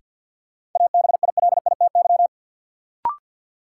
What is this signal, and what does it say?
Morse code, heard as a keyed tone, about 52 wpm, 700 Hz. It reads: ABILITY
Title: Sets of 3 Words - New Word Drill - 50wpm